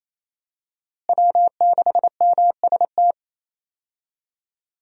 Morse code, heard as a keyed tone, about 28 wpm, 700 Hz. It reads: W6MST
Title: USA Call Signs - 5 Characters - Repeated 3x - 28wpm